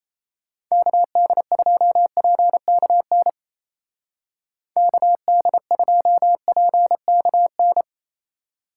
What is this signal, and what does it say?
Morse code, heard as a keyed tone, about 33 wpm, 700 Hz. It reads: KD2PKN KD2PKN
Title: Speed Racing - USA 2x3 Call Signs 28wpm